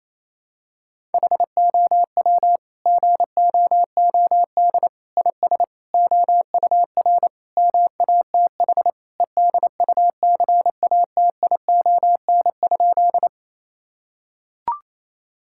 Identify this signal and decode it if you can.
Morse code, heard as a keyed tone, about 28 wpm, 700 Hz. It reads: HOW GOOD IS OUR MATH EDUCATION?
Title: Sentences from Top 1000 Words - Review Part 6 - 28wpm